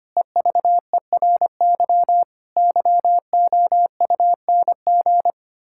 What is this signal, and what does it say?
Morse code, heard as a keyed tone, about 25 wpm, 700 Hz. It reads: EVERY YOUNG
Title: Pangram Challenge v2 - 25wpm